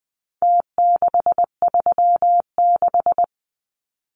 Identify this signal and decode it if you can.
Morse code, heard as a keyed tone, about 20 wpm, 700 Hz. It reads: T636
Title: Numbers - Contest Style - 4-Digit 20wpm